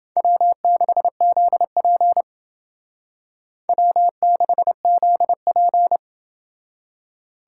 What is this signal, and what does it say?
Morse code, heard as a keyed tone, about 29 wpm, 700 Hz. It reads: W6ZP W6ZP